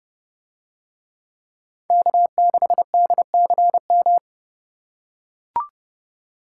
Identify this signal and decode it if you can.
Morse code, heard as a keyed tone, about 29 wpm, 700 Hz. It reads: K6DCM